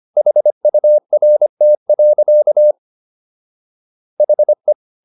Morse code, heard as a keyed tone, about 25 wpm, 600 Hz. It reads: HURT. HE